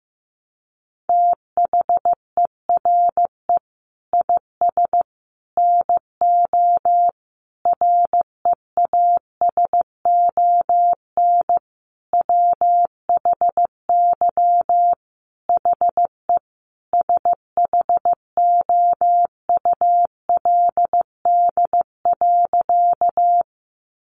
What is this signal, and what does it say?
Morse code, heard as a keyed tone, about 15 wpm, 700 Hz. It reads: THERE IS NO REASON WHY HE SHOULD.
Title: Sentences from Top 700 Words 15wpm